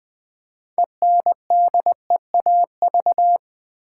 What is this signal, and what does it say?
Morse code, heard as a keyed tone, about 20 wpm, 700 Hz. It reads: ENDEAV
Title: Build A Word - 3 Letters At A Time - 8-Letter Words - 20wpm